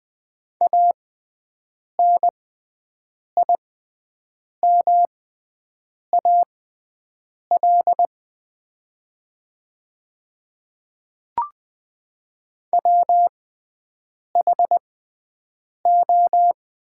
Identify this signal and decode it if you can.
Morse code, heard as a keyed tone, about 20 wpm, spaced as for 7 wpm, 700 Hz. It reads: ANIMAL WHO
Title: 6x Character Spacing - Top 300 Words - 20wpm